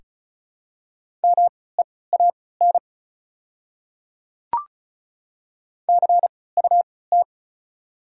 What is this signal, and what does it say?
Morse code, heard as a keyed tone, about 36 wpm, spaced as for 20 wpm, 690 Hz. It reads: MEAN CUT
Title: 3x Character Spacing - Top 300 Words - 35wpm